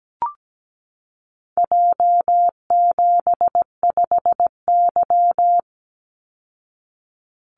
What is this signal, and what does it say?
Morse code, heard as a keyed tone, about 17 wpm, 700 Hz. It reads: J75Y
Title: Call Signs Repeated 3x - 17wpm